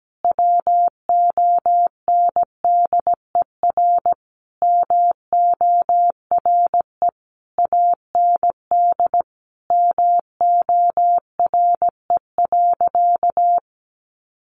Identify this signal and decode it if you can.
Morse code, heard as a keyed tone, about 17 wpm, 700 Hz. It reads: WONDER MORE AND MORE.
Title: Sentences from Top 500 Words 17wpm